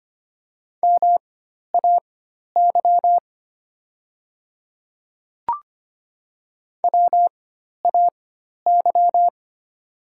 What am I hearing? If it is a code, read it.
Morse code, heard as a keyed tone, about 25 wpm, spaced as for 12 wpm, 690 Hz. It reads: MAY WAY
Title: 4x Character Spacing - Top 200 Words - 25wpm